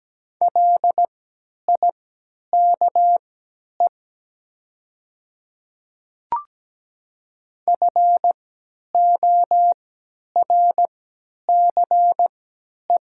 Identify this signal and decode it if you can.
Morse code, heard as a keyed tone, about 17 wpm, spaced as for 10 wpm, 690 Hz. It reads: LIKE FORCE
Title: 3x Character Spacing - Top 500 Words - 17wpm